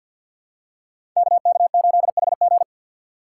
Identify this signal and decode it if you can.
Morse code, heard as a keyed tone, about 48 wpm, 700 Hz. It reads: KK8HG
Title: Code Groups - 5 Letter:Numbers - 50wpm